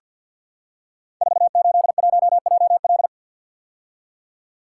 Morse code, heard as a keyed tone, about 47 wpm, 700 Hz. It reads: 481JL
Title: Code Groups - 5 Letter:Numbers - 50wpm